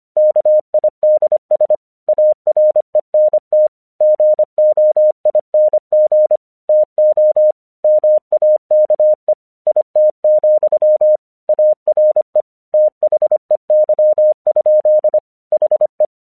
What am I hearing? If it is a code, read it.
Morse code, heard as a keyed tone, about 25 wpm, 610 Hz. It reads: KIDS ARENT GOING TO MAKE IT, ARE THEY? HE
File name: 1 English 25wpm / AQC03